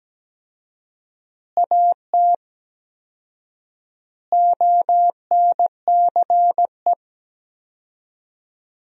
Morse code, heard as a keyed tone, about 17 wpm, 700 Hz. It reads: AT ONCE
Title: 4x Word-Spacing - Sentences from Top 300 Words - 17wpm